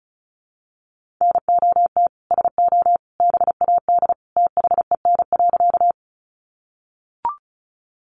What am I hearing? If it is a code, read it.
Morse code, heard as a keyed tone, about 34 wpm, 700 Hz. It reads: NOT SO BAD THEN.
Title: Sentences from Top 700 Words 35wpm